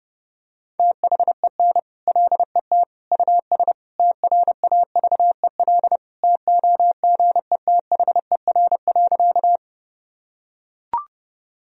Morse code, heard as a keyed tone, about 32 wpm, 700 Hz. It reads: THEN LET US TRAVEL TOGETHER.
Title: Sentences from Top 1000 Words - Review Part 2 - 30wpm